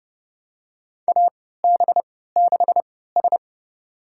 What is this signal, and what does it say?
Morse code, heard as a keyed tone, about 30 wpm, spaced as for 17 wpm, 700 Hz. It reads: AB6S
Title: USA Call Signs - 4 Characters - 3x Character Spacing - 30wpm